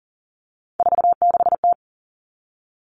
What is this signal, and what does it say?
Morse code, heard as a keyed tone, about 41 wpm, 700 Hz. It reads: 46T
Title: Three Challenging Letter-Number Combinations 40wpm